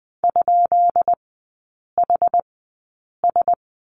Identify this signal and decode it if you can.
Morse code, heard as a keyed tone, about 20 wpm, spaced as for 8 wpm, 690 Hz. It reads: ?HS
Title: 159 Lesson 26e - Letter g - Three Characters 20wpm